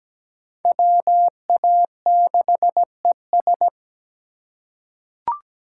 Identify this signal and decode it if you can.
Morse code, heard as a keyed tone, about 17 wpm, 700 Hz. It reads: WA6ES
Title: USA Call Signs - 5 Characters - 3x Character Spacing - 17wpm